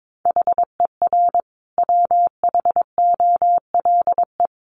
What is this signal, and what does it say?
Morse code, heard as a keyed tone, about 22 wpm, 700 Hz. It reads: HER WHOLE
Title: Sentences from Top 1500 Words 22wpm